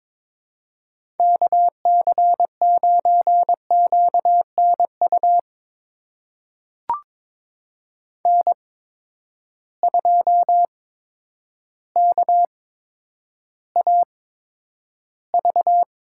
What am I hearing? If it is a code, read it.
Morse code, heard as a keyed tone, about 21 wpm, spaced as for 10 wpm, 700 Hz. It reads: KC9QNU N2KAV
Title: USA Call Signs - 8x Character Spacing - 22wpm